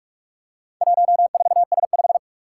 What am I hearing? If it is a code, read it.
Morse code, heard as a keyed tone, about 46 wpm, 700 Hz. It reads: 14S5
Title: Four Challenging Letter-Number Combinations 45wpm